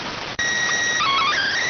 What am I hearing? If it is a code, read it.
SSTV (slow-scan television), PD 290.